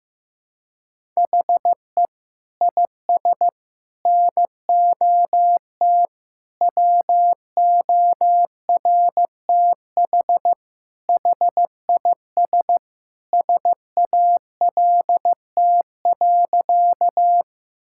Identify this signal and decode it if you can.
Morse code, heard as a keyed tone, about 15 wpm, 700 Hz. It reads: HE IS NOT WORTH HIS SALT.
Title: Sentences from Top 1000 Words - Review Part 5 - 15wpm